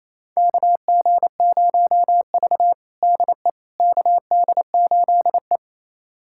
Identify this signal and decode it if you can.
Morse code, heard as a keyed tone, about 28 wpm, 700 Hz. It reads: KG0V DE KD8E
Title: USA Callsign DE Callsign - 4 Characters - 28wpm